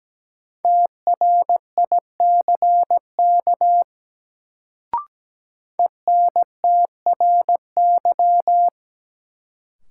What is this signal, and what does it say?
Morse code, heard as a keyed tone, about 17 wpm, 700 Hz. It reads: TRICK ENTRY
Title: Sentences from Top 1500 Words - New Word Drill Alt - 17wpm